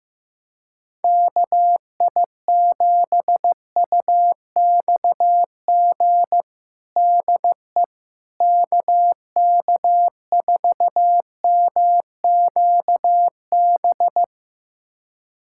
Morse code, heard as a keyed tone, about 16 wpm, 700 Hz. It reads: KI7UXG DE KK4MQB
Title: USA Callsign DE Callsign - 6 Characters - 15wpm